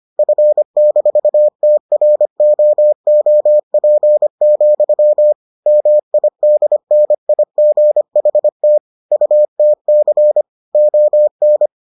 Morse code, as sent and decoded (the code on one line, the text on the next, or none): ..-. -....- - .-. --- --- .--. --..-- / -- .. -.. -. .. --. .... - / ..- - -.-. / --- -.
F-TROOP, MIDNIGHT UTC ON